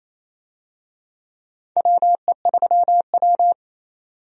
Morse code, heard as a keyed tone, about 28 wpm, 700 Hz. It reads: WE3W
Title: USA Call Signs - 4 Characters - 28wpm